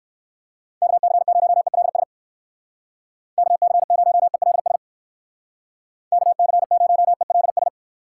Morse code, heard as a keyed tone, about 59 wpm, 700 Hz. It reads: KC0ELS KC0ELS KC0ELS